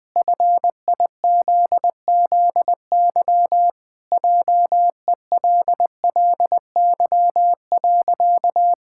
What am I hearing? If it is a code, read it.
Morse code, heard as a keyed tone, about 20 wpm, 700 Hz. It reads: FIZZY JELLY.
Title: Pangram Challenge v2 - 20wpm